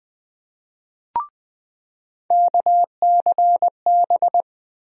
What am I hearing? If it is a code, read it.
Morse code, heard as a keyed tone, about 21 wpm, 700 Hz. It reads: KCB
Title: USA Call Sign Postfixes - 3 Letters 20wpm